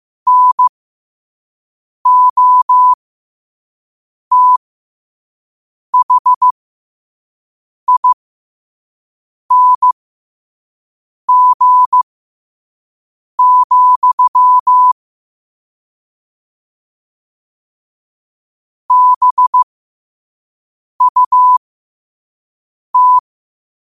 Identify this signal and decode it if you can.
Morse code, heard as a keyed tone, about 15 wpm, spaced as for 5 wpm, 1000 Hz. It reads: NOTHING, BUT
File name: qotdfiles / 5wpm